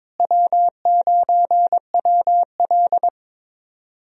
Morse code, heard as a keyed tone, about 22 wpm, 700 Hz. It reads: W9WL
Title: USA Call Signs - 4 Characters - Repeated 3x - 22wpm